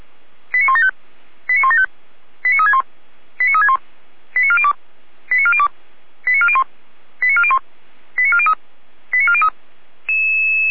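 Five-tone selective call (ZVEI), heard as quick runs of five tones, tones of about 70 ms. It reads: 89175 89175 89361 89361 89402 89402 89401 89401 89403 89403
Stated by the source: ZVEI five-tone call 89356 received on SDR.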